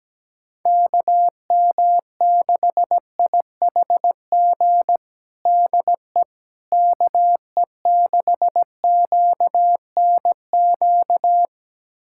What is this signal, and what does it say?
Morse code, heard as a keyed tone, about 17 wpm, 700 Hz. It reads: KM6IHG DE KE6QNQ
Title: USA Callsign DE Callsign - 6 Characters - 17wpm